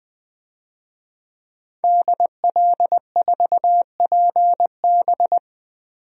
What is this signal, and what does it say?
Morse code, heard as a keyed tone, about 20 wpm, 700 Hz. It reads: DL4PB
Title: DXCC-100 Call Signs - 5 Characters - Repeated 2x - 20wpm